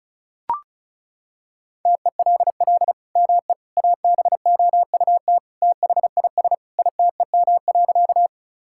Morse code, heard as a keyed tone, about 34 wpm, 700 Hz. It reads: TELL ME ABOUT THIS ITEM.
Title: Sentences from Top 1000 Words - Review Part 6 - 35wpm